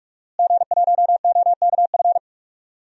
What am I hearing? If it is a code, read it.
Morse code, heard as a keyed tone, about 46 wpm, 700 Hz. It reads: G1OKF
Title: DXCC-100 Call Signs - 5 Characters - Repeated 2x - 45wpm